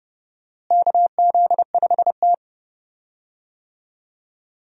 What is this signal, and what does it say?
Morse code, heard as a keyed tone, about 30 wpm, 700 Hz. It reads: KZ5T
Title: USA Call Signs - 4 Characters - Repeated 2x - 30wpm